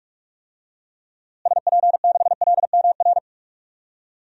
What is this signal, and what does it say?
Morse code, heard as a keyed tone, about 46 wpm, 700 Hz. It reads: SP6LMR
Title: DXCC-100 Call Signs - 6 Characters - Repeated 2x - 45wpm